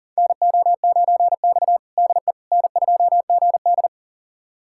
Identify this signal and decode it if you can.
Morse code, heard as a keyed tone, about 40 wpm, 700 Hz. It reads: NO9X DE N2GD
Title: USA Callsign DE Callsign - 4 Characters - 40wpm